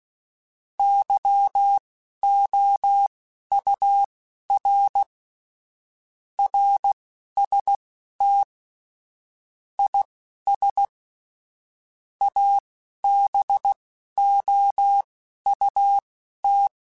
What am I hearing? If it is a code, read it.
Morse code, heard as a keyed tone, about 16 wpm, spaced as for 11 wpm, 800 Hz. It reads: YOUR RST IS ABOUT
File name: QSOs / QSOs-mp3s / WD0FBT 1@10